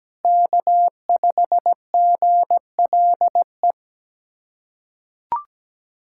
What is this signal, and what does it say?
Morse code, heard as a keyed tone, about 17 wpm, 700 Hz. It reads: K5GLE
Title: USA Callsign DE Callsign 17wpm